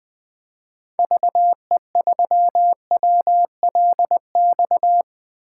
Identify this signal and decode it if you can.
Morse code, heard as a keyed tone, about 20 wpm, 700 Hz. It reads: VE3WLX
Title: 2023 POTA Activator Call Signs - 20wpm